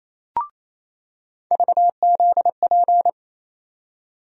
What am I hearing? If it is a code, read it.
Morse code, heard as a keyed tone, about 28 wpm, 700 Hz. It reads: VZP
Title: USA Call Sign Postfixes - 3 Letters 28wpm